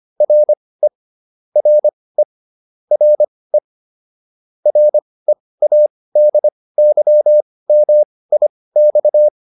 Morse code, heard as a keyed tone, about 25 wpm, spaced as for 18 wpm, 600 Hz. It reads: RE RE RE READYMIX